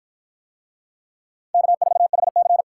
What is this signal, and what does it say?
Morse code, heard as a keyed tone, about 53 wpm, 700 Hz. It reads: K4HC